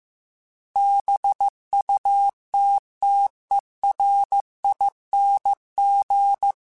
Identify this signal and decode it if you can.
Morse code, heard as a keyed tone, about 15 wpm, 800 Hz. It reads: BUTTERING
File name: Special INGs-mp3s / Special INGs-mp3s / Sngl ing @13